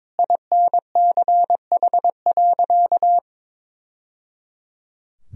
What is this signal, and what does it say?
Morse code, heard as a keyed tone, about 22 wpm, 700 Hz. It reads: INCH.
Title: Sentences from Top 1000 Words - Review Part 3 - 22wpm